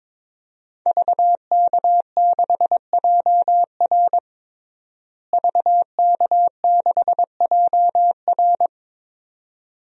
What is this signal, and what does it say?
Morse code, heard as a keyed tone, about 21 wpm, 700 Hz. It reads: VK6JR VK6JR
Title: DXCC-100 Call Signs - 5 Characters - Repeated 2x - 22wpm